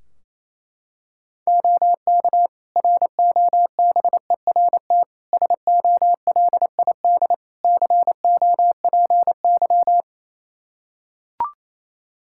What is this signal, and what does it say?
Morse code, heard as a keyed tone, about 28 wpm, 700 Hz. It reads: OK ROBERT SOLID COPY
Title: K1USN SST QSO Practice 28wpm